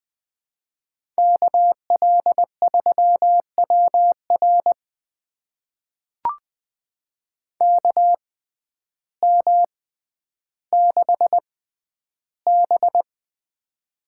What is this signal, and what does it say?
Morse code, heard as a keyed tone, about 20 wpm, spaced as for 11 wpm, 700 Hz. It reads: KL3WR KM6B
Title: USA Call Signs - 6x Character Spacing - 20wpm